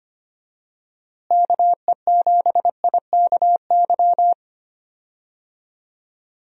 Morse code, heard as a keyed tone, about 25 wpm, 700 Hz. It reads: KE7IKY
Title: USA Call Signs - 6 Characters - Repeated 2x - 25wpm